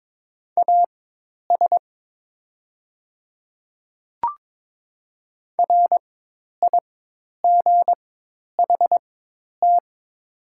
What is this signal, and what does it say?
Morse code, heard as a keyed tone, about 22 wpm, spaced as for 10 wpm, 700 Hz. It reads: AS RIGHT
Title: 4x Character Spacing - Top 200 Words - 22wpm